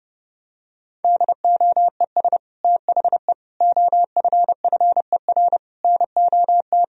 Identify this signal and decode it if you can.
Morse code, heard as a keyed tone, about 32 wpm, 700 Hz. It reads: DOES THE OFFER NOT
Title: Sentences from Top 1000 Words - Review Part 4 - 30wpm